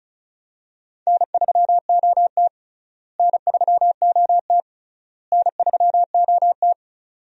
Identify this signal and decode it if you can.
Morse code, heard as a keyed tone, about 36 wpm, 700 Hz. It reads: N3OT N3OT N3OT
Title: USA Call Signs - 4 Characters - Repeated 3x - 35wpm